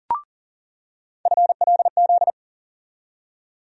Morse code, heard as a keyed tone, about 40 wpm, 700 Hz. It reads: FLZ
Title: USA Call Sign Postfixes - 3 Letters 40wpm